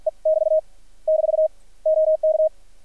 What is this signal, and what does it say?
Morse code, heard as a keyed tone, about 50 wpm, 630 Hz. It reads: E <BT> - OK
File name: archived / CRSnet